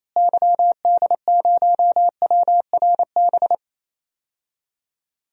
Morse code, heard as a keyed tone, about 28 wpm, 700 Hz. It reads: YD0WRB
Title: DXCC-100 Call Signs - 6 Characters - Repeated 3x - 28wpm